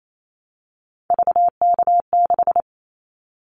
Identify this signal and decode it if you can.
Morse code, heard as a keyed tone, about 28 wpm, 700 Hz. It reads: VK6
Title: Callsign Prefixes for Top 100 DXCC - 28wpm